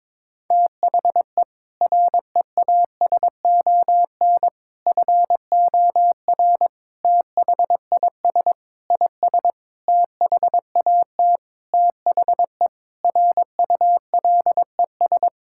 Morse code, heard as a keyed tone, about 22 wpm, 700 Hz. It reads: THE REASON FOR THIS IS THAT THE RULES
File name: Sentences with 7-Word Phrases - 22wpm